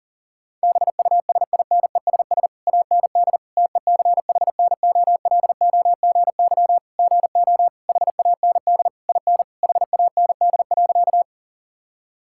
Morse code, heard as a keyed tone, about 40 wpm, 700 Hz. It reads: BUSINESS AND TECHNOLOGY GO HAND IN HAND.
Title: Sentences from Top 1000 Words - Review Part 5 - 40wpm